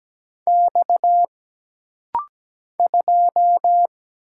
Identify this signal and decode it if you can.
Morse code, heard as a keyed tone, about 17 wpm, 700 Hz. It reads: X 2
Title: Single Letter-Number - Rapid-Fire 17wpm